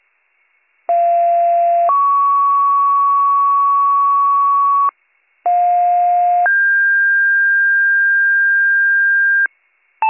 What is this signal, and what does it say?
Two-tone page, heard as a one-second tone, then a three-second tone, about 690 then 1120 Hz, about 710 then 1600 Hz, with noise. It is heard from 0.9 to 9.5 s.